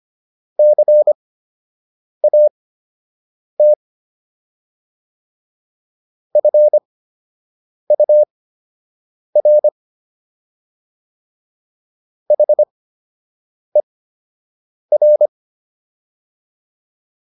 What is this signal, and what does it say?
Morse code, heard as a keyed tone, about 25 wpm, spaced as for 7 wpm, 600 Hz. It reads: CAT FUR HER